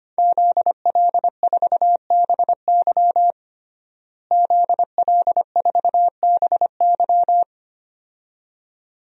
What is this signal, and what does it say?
Morse code, heard as a keyed tone, about 25 wpm, 700 Hz. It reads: ZL4BY ZL4BY
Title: DXCC-100 Call Signs - 5 Characters - Repeated 2x - 25wpm